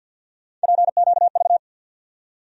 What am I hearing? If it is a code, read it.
Morse code, heard as a keyed tone, about 53 wpm, 700 Hz. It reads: PQV
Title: USA Call Sign Postfixes - 3 Letters 50wpm